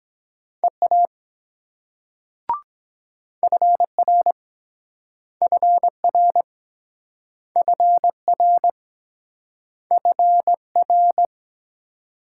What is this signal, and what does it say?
Morse code, heard as a keyed tone, about 21 wpm, 700 Hz. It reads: EA FR FR FR FR
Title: Speed-Racing Multi-letter Phonemes - 17wpm